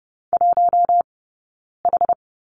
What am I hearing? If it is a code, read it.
Morse code, heard as a keyed tone, about 30 wpm, 700 Hz. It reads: 1 H
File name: Lesson 17d - Letter u - Two Characters 30wpm